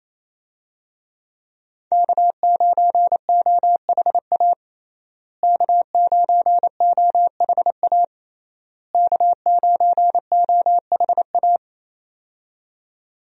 Morse code, heard as a keyed tone, about 27 wpm, 700 Hz. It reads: K9OHA K9OHA K9OHA
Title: USA Call Signs - 5 Characters - Repeated 3x - 28wpm